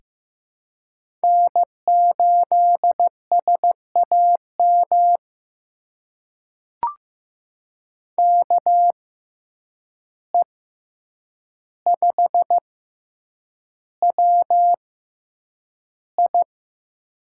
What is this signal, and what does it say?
Morse code, heard as a keyed tone, about 15 wpm, spaced as for 8 wpm, 700 Hz. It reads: N8SAM KE5WI
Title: USA Call Signs - 5 Characters - 6x Character Spacing - 15wpm